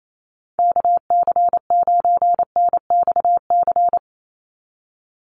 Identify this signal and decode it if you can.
Morse code, heard as a keyed tone, about 28 wpm, 700 Hz. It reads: KC9NXC
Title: USA Call Signs 28wpm